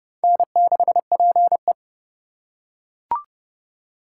Morse code, heard as a keyed tone, about 28 wpm, 700 Hz. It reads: N6PE